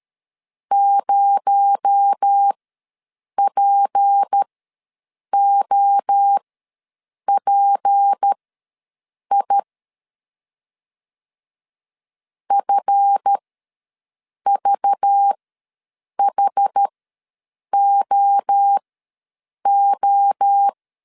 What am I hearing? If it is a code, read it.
Morse code, heard as a keyed tone, about 13 wpm, spaced as for 7 wpm, 800 Hz. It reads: 0POPI FVHOO
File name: LEZIONI / 8 YJ45 / 8 Y ws 12wpm 3409